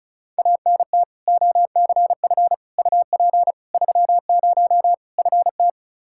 Morse code, heard as a keyed tone, about 36 wpm, 700 Hz. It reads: ANT OCF UP 30 FT
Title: Practice QSO - Exchange 2 - Info - 35wpm